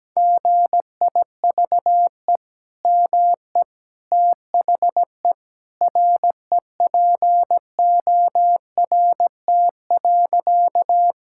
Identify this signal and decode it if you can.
Morse code, heard as a keyed tone, about 17 wpm, 700 Hz. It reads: GIVE ME THE REPORT.
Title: Sentences from Top 1000 Words - Review Part 6 - 17wpm